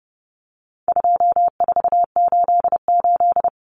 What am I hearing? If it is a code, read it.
Morse code, heard as a keyed tone, about 30 wpm, 700 Hz. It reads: 2488
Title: Common Cut Numbers - 4-Digit 30wpm